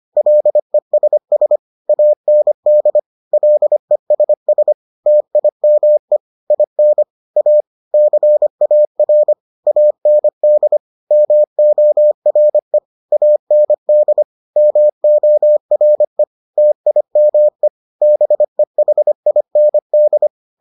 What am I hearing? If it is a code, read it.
Morse code, heard as a keyed tone, about 25 wpm, 600 Hz. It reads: LESS AND LESS TIME IN A CAR AND MORE AND MORE TIME BEHIND